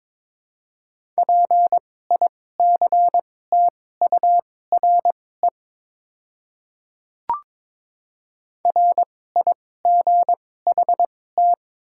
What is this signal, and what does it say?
Morse code, heard as a keyed tone, about 21 wpm, spaced as for 16 wpm, 700 Hz. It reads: PICTURE RIGHT
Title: 2x Character Spacing - Top 200 Words - 22wpm